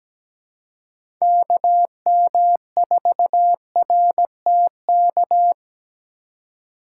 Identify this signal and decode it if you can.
Morse code, heard as a keyed tone, about 17 wpm, 700 Hz. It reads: KM4RTK
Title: USA Call Signs - 6 Characters - 3x Character Spacing - 17wpm